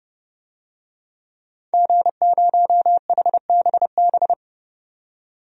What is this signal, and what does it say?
Morse code, heard as a keyed tone, about 30 wpm, 700 Hz. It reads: G0HBB
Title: DXCC-100 Call Signs - 5 Characters - 30wpm